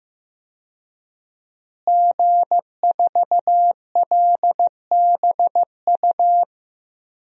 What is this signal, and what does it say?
Morse code, heard as a keyed tone, about 15 wpm, 700 Hz. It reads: G4LBU
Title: DXCC-100 Call Signs - 5 Characters - Repeated 3x - 15wpm